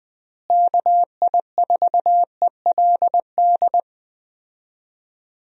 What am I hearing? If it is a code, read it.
Morse code, heard as a keyed tone, about 20 wpm, 700 Hz. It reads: KI4ELD
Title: USA Call Signs - 6 Characters - Repeated 3x - 20wpm